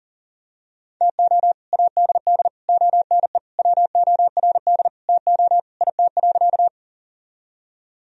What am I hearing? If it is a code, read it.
Morse code, heard as a keyed tone, about 40 wpm, 700 Hz. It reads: TO ADD ONE WORD TO IT.
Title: Sentences from Top 1000 Words - Review Part 1 - 40wpm